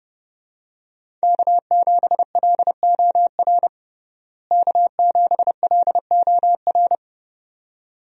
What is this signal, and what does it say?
Morse code, heard as a keyed tone, about 32 wpm, 700 Hz. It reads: K7LOR K7LOR
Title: USA Call Signs - 5 Characters - Repeated 2x - 30wpm